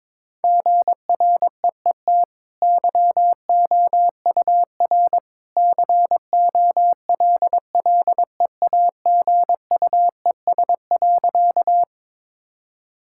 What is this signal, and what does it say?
Morse code, heard as a keyed tone, about 22 wpm, 700 Hz. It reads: GREET YOUR COLLEAGUES.
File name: practice / 22wpm / Easy Sentences with Hints - 22wpm